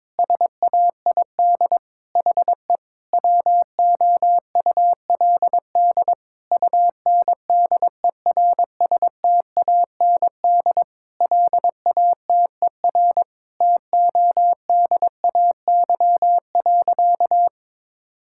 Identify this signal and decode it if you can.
Morse code, heard as a keyed tone, about 22 wpm, 700 Hz. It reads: SAID HE WOULD UNDERSTAND LATER TODAY.
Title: Sentences from Top 1000 Words - Review Part 5 - 22wpm